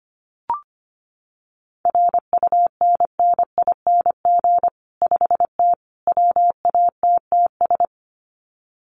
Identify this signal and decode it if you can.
Morse code, heard as a keyed tone, about 25 wpm, 700 Hz. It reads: RUNNING 5T WATTS
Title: Common QSO Phrases 25wpm